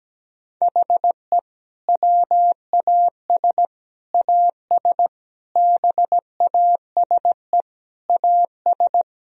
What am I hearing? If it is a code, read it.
Morse code, heard as a keyed tone, about 17 wpm, 700 Hz. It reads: HE WAS AS BASE AS
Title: Sentences from Top 1000 Words - Review Part 1 - 17wpm